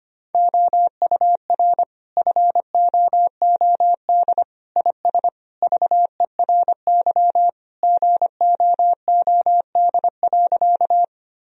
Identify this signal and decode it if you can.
Morse code, heard as a keyed tone, about 25 wpm, 700 Hz. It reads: OUR FOOD IS VERY GOOD.
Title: Sentences from Top 1000 Words - Review Part 1 - 25wpm